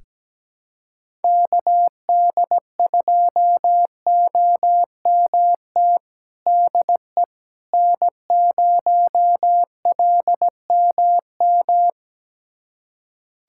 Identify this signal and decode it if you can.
Morse code, heard as a keyed tone, about 17 wpm, 700 Hz. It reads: KD2OMT DE N0LMM
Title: USA Callsign DE Callsign 17wpm